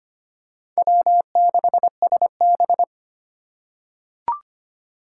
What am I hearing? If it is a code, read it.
Morse code, heard as a keyed tone, about 25 wpm, 700 Hz. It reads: W6SB
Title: USA Call Signs - 3x Character Spacing - 25wpm